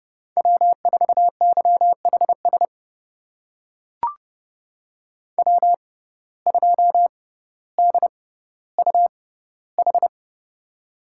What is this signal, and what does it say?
Morse code, heard as a keyed tone, about 29 wpm, spaced as for 15 wpm, 700 Hz. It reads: W4YHS W2DUH
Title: USA Call Signs - 5 Characters - 6x Character Spacing - 30wpm